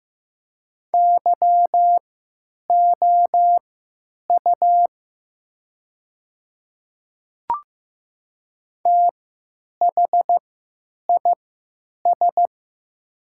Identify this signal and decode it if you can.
Morse code, heard as a keyed tone, about 15 wpm, spaced as for 8 wpm, 700 Hz. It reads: YOU THIS
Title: 3x Character Spacing - Top 100 Words - 15wpm